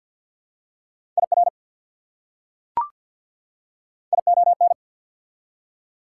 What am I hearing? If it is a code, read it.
Morse code, heard as a keyed tone, about 48 wpm, 690 Hz. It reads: IR ION
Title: Prefixes and Suffixes 50wpm